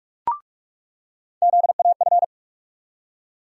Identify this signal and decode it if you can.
Morse code, heard as a keyed tone, about 46 wpm, 700 Hz. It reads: ZAF
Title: USA Call Sign Postfixes - 3 Letters 45wpm